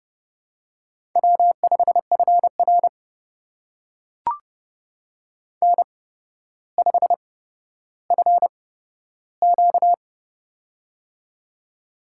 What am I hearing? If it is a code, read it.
Morse code, heard as a keyed tone, about 29 wpm, spaced as for 13 wpm, 700 Hz. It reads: W5FR N5FQ
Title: USA Call Signs - 4 Characters - 8x Character Spacing - 30wpm